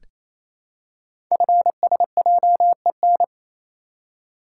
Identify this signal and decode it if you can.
Morse code, heard as a keyed tone, about 29 wpm, 700 Hz. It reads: FSJEN
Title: Code Groups - 5 Letters - 28wpm